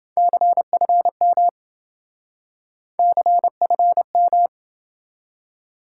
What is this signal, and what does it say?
Morse code, heard as a keyed tone, about 27 wpm, 700 Hz. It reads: CFM CFM